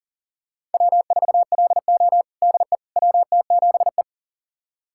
Morse code, heard as a keyed tone, about 40 wpm, 700 Hz. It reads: W4LO DE WT7E
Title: USA Callsign DE Callsign - 4 Characters - 40wpm